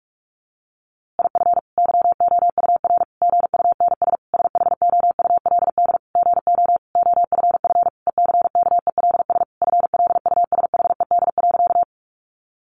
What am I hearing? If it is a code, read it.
Morse code, heard as a keyed tone, about 44 wpm, 700 Hz. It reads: IF YOUR GUNS SHOULD GO OFF ECKELS FLUSHED.
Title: A Sound Of Thunder 45wpm